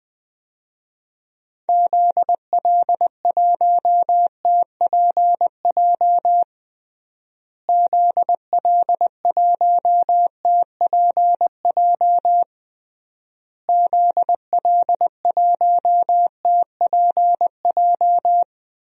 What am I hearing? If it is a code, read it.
Morse code, heard as a keyed tone, about 20 wpm, 700 Hz. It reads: ZL1TPJ ZL1TPJ ZL1TPJ
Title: DXCC-100 Call Signs - 6 Characters - Repeated 3x - 20wpm